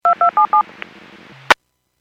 DTMF, heard as telephone touch tones, tones of about 110 ms. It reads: 2277
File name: Businesses / 03091 23-58 Businesses 451.25 WPS